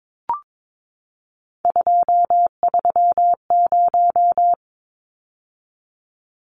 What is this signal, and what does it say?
Morse code, heard as a keyed tone, about 22 wpm, 700 Hz. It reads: 230
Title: Numbers - 3-Digit 22wpm